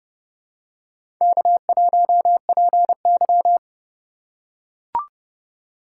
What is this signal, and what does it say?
Morse code, heard as a keyed tone, about 30 wpm, 700 Hz. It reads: K1PY
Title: USA Call Signs - 4 Characters - 4x Character Spacing - 30wpm